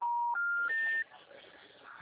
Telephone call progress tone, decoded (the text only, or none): special information tone